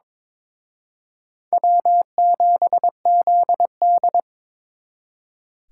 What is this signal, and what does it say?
Morse code, heard as a keyed tone, about 26 wpm, 700 Hz. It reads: W7ZD
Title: Speed Racing - USA Call Signs 22wpm